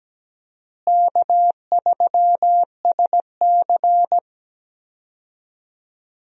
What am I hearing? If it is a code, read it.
Morse code, heard as a keyed tone, about 17 wpm, 700 Hz. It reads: K3SC
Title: USA Call Signs - 4 Characters - Repeated 2x - 17wpm